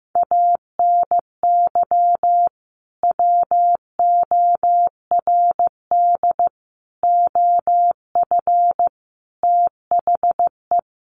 Morse code, heard as a keyed tone, about 15 wpm, 700 Hz. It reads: ANY WORD OF THE
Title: Sentences from Top 1300 Words 15wpm